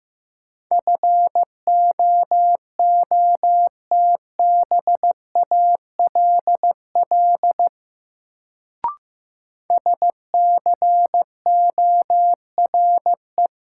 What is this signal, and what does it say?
Morse code, heard as a keyed tone, about 15 wpm, 700 Hz. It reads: FOOTBALL SCORE
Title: Sentences from Top 1600 Words - New Word Drill Alt - 15wpm